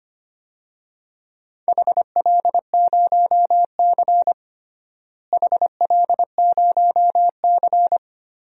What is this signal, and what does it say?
Morse code, heard as a keyed tone, about 25 wpm, 700 Hz. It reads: HL0C HL0C
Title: DXCC-100 Call Signs - 4 Characters - Repeated 3x - 25wpm